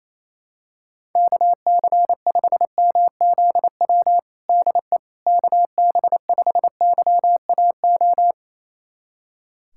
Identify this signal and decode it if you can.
Morse code, heard as a keyed tone, about 28 wpm, 700 Hz. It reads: KC5MZW DE KB5YAO
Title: USA Callsign DE Callsign - 6 Characters - 28wpm